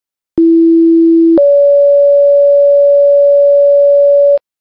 Two-tone page, heard as a one-second tone, then a three-second tone, about 330 then 570 Hz. It is heard from 0.4 to 4.4 s.